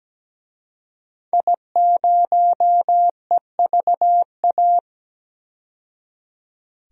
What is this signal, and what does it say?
Morse code, heard as a keyed tone, about 17 wpm, 700 Hz. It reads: I0EVA
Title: DXCC-100 Call Signs - 5 Characters - 17wpm